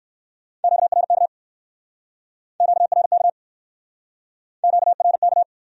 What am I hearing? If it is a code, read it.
Morse code, heard as a keyed tone, about 66 wpm, 700 Hz. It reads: QRK QRK QRK